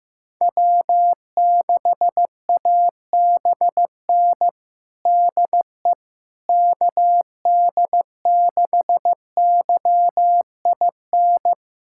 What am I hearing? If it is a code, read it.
Morse code, heard as a keyed tone, about 15 wpm, 700 Hz. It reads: W6ABN DE KD6YIN
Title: USA Callsign DE Callsign 15wpm